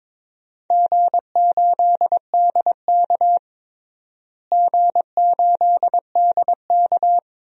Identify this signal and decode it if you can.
Morse code, heard as a keyed tone, about 21 wpm, 700 Hz. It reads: G8DK G8DK
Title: DXCC-100 Call Signs - 4 Characters - Repeated 3x - 22wpm